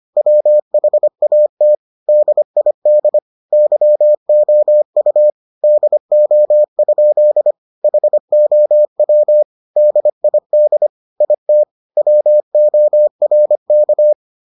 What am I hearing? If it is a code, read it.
Morse code, heard as a keyed tone, about 25 wpm, 600 Hz. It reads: WHAT DID YOU DO? HOW DID IT WORK